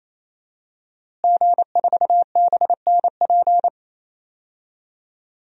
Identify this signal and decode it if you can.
Morse code, heard as a keyed tone, about 28 wpm, 700 Hz. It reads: G4BNP
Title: DXCC-100 Call Signs - 5 Characters - Repeated 3x - 28wpm